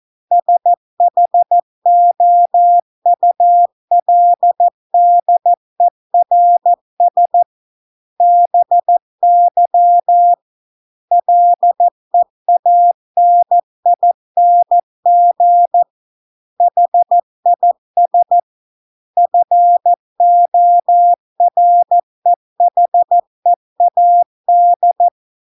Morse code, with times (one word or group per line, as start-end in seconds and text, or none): 0.3-7.4 SHOULDERS
8.2-10.3 BY
11.1-15.8 LEANING
16.6-18.4 HIS
19.2-25.1 FOREHEAD